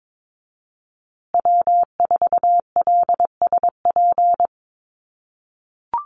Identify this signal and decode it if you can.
Morse code, heard as a keyed tone, about 22 wpm, 700 Hz. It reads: W4LSP
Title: USA Call Signs Repeated 2x 22wpm